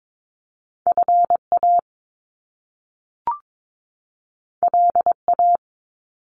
Morse code, heard as a keyed tone, about 22 wpm, 700 Hz. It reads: FA LA
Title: Top Two Letter Combinations in Words 22wpm